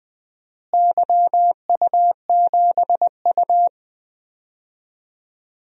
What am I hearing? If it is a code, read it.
Morse code, heard as a keyed tone, about 20 wpm, 700 Hz. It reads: YU7U
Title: DXCC-100 Call Signs - 4 Characters - Repeated 2x - 20wpm